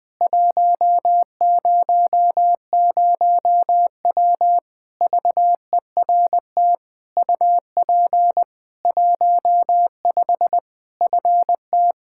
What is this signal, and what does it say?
Morse code, heard as a keyed tone, about 20 wpm, 700 Hz. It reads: 100W VERT UP 15 FT
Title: K1USN SST QSO Practice 20wpm